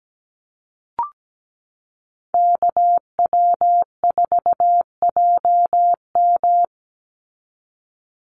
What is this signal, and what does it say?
Morse code, heard as a keyed tone, about 17 wpm, 700 Hz. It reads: KW4JM
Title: NA SOTA Activator Callsigns 17wpm